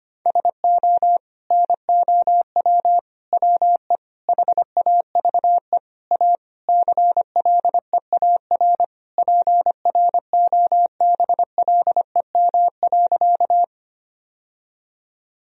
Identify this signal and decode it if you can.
Morse code, heard as a keyed tone, about 25 wpm, 700 Hz. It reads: SO NOW WE HAVE A CLEAR PROBLEM.
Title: Sentences from Top 1000 Words - Review Part 2 - 25wpm